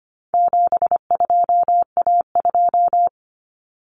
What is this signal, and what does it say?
Morse code, heard as a keyed tone, about 25 wpm, 700 Hz. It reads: 72A2
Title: Common Cut Numbers - 4-Digit 25wpm